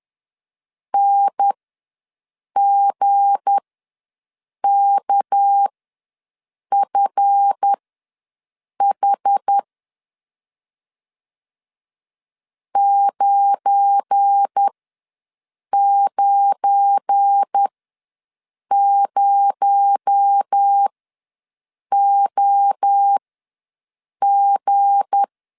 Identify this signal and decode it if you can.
Morse code, heard as a keyed tone, about 10 wpm, spaced as for 6 wpm, 800 Hz. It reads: NGKFH 990OG
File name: LEZIONI / 7 GF9Z / 7 GF9 ws 10wpm 4034